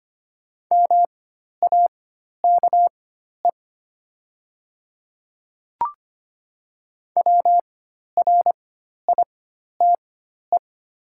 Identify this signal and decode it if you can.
Morse code, heard as a keyed tone, about 25 wpm, spaced as for 12 wpm, 700 Hz. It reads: MAKE WRITE
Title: 4x Character Spacing - Top 100 Words - 25wpm